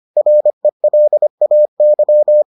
Morse code, heard as a keyed tone, about 25 wpm, 600 Hz. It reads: RELAY